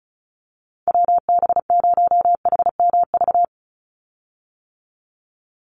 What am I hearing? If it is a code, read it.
Morse code, heard as a keyed tone, about 34 wpm, 700 Hz. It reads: WB0HMV
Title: USA Call Signs 35wpm